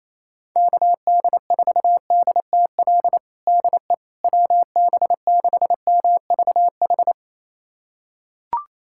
Morse code, heard as a keyed tone, about 28 wpm, 700 Hz. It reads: KD4DTL DE WB6MVH
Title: USA Callsign DE Callsign - 6 Characters - 28wpm